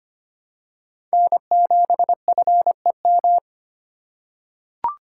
Morse code, heard as a keyed tone, about 25 wpm, 700 Hz. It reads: N7FEM
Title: USA Call Signs - 3x Character Spacing - 25wpm